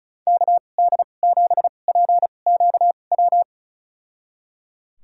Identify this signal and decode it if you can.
Morse code, heard as a keyed tone, about 36 wpm, spaced as for 25 wpm, 700 Hz. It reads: KD7PQW
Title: USA Call Signs - 6 Characters - 2x Character Spacing - 35wpm